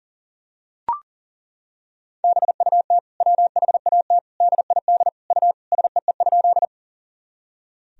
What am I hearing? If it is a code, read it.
Morse code, heard as a keyed tone, about 40 wpm, 700 Hz. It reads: BUT WHAT DID U SEE?
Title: Sentences from Top 100 Words and Abbreviations 40wpm